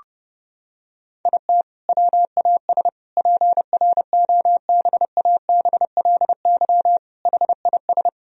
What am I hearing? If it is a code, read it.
Morse code, heard as a keyed tone, about 32 wpm, 700 Hz. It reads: IT WAS PROBABLY HIS
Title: Sentences from Top 1000 Words - Review Part 4 - 30wpm